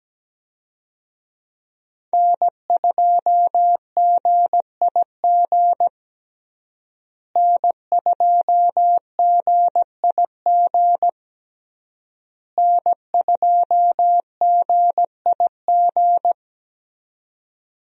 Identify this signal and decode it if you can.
Morse code, heard as a keyed tone, about 17 wpm, 700 Hz. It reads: N2GIG N2GIG N2GIG
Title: USA Call Signs - 5 Characters - Repeated 3x - 17wpm